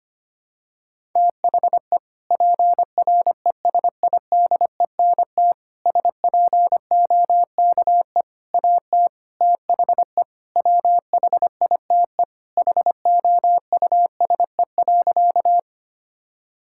Morse code, heard as a keyed tone, about 25 wpm, 700 Hz. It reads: THE PRESIDENT SPOKE AT THE WHITE HOUSE.
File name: practice / 25wpm / Sentences with 4-Word Phrases - 25wpm